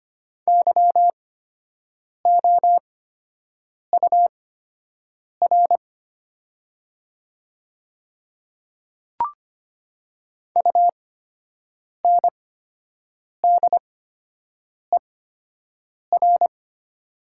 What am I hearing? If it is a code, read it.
Morse code, heard as a keyed tone, about 25 wpm, spaced as for 7 wpm, 700 Hz. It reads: YOUR UNDER
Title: 8x Character Spacing - Top 200 Words - 25wpm